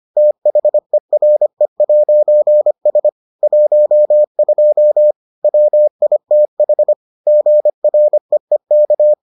Morse code, heard as a keyed tone, about 25 wpm, 600 Hz. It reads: THERE'S 12 WITH GREEK